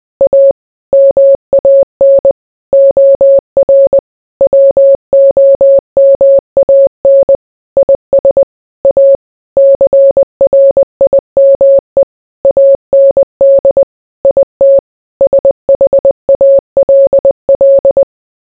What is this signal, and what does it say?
Morse code, heard as a keyed tone, about 20 wpm, 550 Hz. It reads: A MAN OR WOMAN IS A CRIME AND IT SHALL